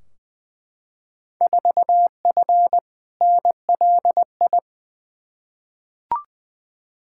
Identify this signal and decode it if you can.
Morse code, heard as a keyed tone, about 20 wpm, 700 Hz. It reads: 4F NLI
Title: Field Day Exchanges Encore - 20wpm